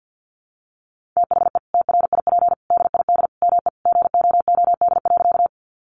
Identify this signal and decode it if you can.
Morse code, heard as a keyed tone, about 50 wpm, 700 Hz. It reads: THE TRIP DID ME GOOD.
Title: Sentences from Top 1300 Words 50wpm